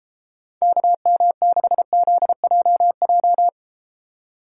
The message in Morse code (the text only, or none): KM6ZJJ